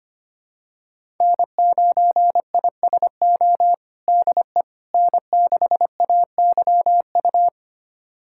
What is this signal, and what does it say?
Morse code, heard as a keyed tone, about 25 wpm, 700 Hz. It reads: N9ISO DE N6AYU
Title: USA Callsign DE Callsign - 5 Characters - 25wpm